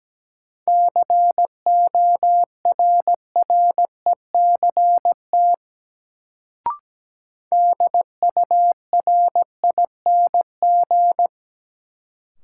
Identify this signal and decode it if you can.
Morse code, heard as a keyed tone, about 17 wpm, 700 Hz. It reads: CORRECT DURING
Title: Sentences from Top 700 Words - New Word Drill - 17wpm